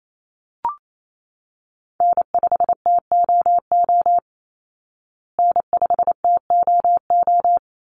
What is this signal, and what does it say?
Morse code, heard as a keyed tone, about 28 wpm, 700 Hz. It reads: N5TOO N5TOO
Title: Call Signs Repeated 3x - 28wpm